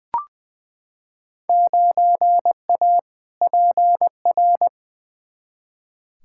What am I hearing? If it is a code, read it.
Morse code, heard as a keyed tone, about 21 wpm, 700 Hz. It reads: 9A PR
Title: Field Day Exchanges Encore - 20wpm